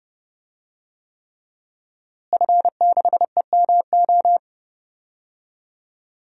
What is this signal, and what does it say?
Morse code, heard as a keyed tone, about 32 wpm, 700 Hz. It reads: F6EMO
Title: DXCC-100 Call Signs - 5 Characters - Repeated 3x - 30wpm